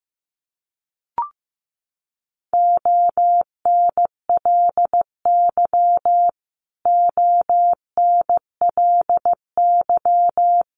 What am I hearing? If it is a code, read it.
Morse code, heard as a keyed tone, about 15 wpm, 700 Hz. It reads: ONLY ONLY
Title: Top 500 Words Repeated 15wpm